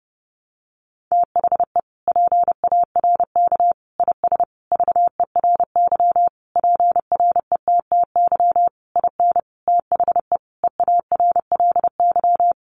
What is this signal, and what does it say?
Morse code, heard as a keyed tone, about 30 wpm, 700 Hz. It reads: THE PARK IS VERY PRETTY IN THE EARLY
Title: Sentences from Top 1300 Words 30wpm